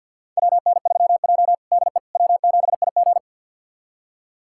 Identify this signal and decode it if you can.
Morse code, heard as a keyed tone, about 46 wpm, 700 Hz. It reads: WN3J DE W7IG
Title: USA Callsign DE Callsign - 4 Characters - 50wpm